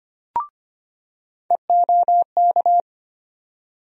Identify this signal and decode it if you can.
Morse code, heard as a keyed tone, about 25 wpm, 700 Hz. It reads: EOK
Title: USA Call Sign Postfixes - 3 Letters 25wpm